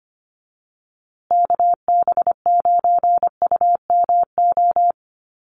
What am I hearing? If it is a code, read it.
Morse code, heard as a keyed tone, about 25 wpm, 700 Hz. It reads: KB9UMO
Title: USA Call Signs Repeated 2x 25wpm